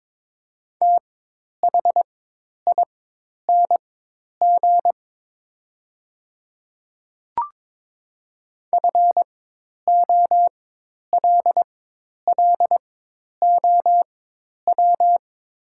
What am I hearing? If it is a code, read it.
Morse code, heard as a keyed tone, about 22 wpm, spaced as for 10 wpm, 700 Hz. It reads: THING FOLLOW
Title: 4x Character Spacing - Top 200 Words - 22wpm